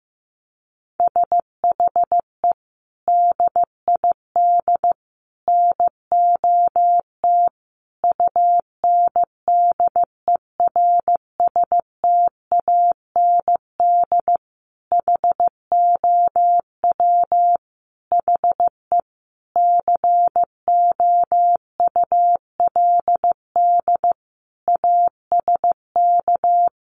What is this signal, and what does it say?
Morse code, heard as a keyed tone, about 15 wpm, 700 Hz. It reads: SHE DID NOT UNDERSTAND HOW HE COULD ASK
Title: Sentences from Top 600 Words 15wpm